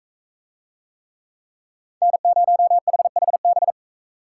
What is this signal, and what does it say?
Morse code, heard as a keyed tone, about 44 wpm, 700 Hz. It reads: N0HHB